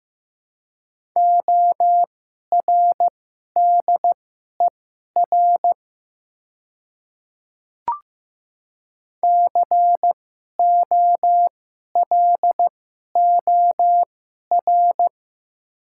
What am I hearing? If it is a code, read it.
Morse code, heard as a keyed tone, about 15 wpm, spaced as for 11 wpm, 700 Hz. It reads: ORDER COLOR
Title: 2x Character Spacing - Top 500 Words - 15wpm